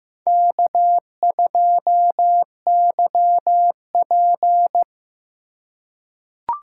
Morse code, heard as a keyed tone, about 15 wpm, 700 Hz. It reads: K2YP
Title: USA Call Signs - 4x Character Spacing - 15wpm